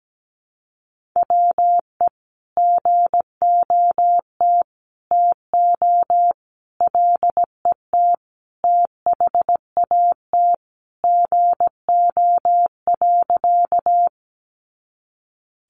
Morse code, heard as a keyed tone, about 17 wpm, 700 Hz. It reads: WE GOT TO LET THAT GO.
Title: Sentences from Top 300 Words 17wpm